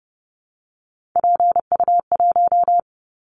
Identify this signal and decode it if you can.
Morse code, heard as a keyed tone, about 30 wpm, 700 Hz. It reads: PU1
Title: Callsign Prefixes for Top 100 DXCC - 30wpm